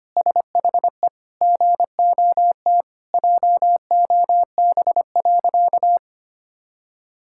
Morse code, heard as a keyed tone, about 25 wpm, 700 Hz. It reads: SHE GOT JOB.
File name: practice / 25wpm / Sentences with 4-Word Phrases - 25wpm